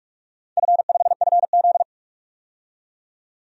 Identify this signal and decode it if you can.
Morse code, heard as a keyed tone, about 44 wpm, 700 Hz. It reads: F5FZ
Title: DXCC-100 Call Signs - 4 Characters - 45wpm